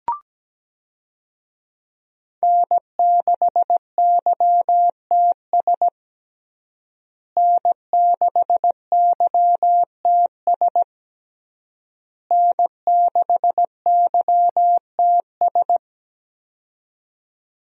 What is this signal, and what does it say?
Morse code, heard as a keyed tone, about 17 wpm, 700 Hz. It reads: N6YTS N6YTS N6YTS
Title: USA Call Signs - 5 Characters - Repeated 3x - 17wpm